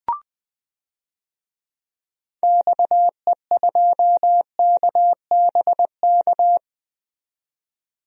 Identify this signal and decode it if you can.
Morse code, heard as a keyed tone, about 20 wpm, 700 Hz. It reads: XE2KBK
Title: DXCC-100 Call Signs - 6 Characters - 20wpm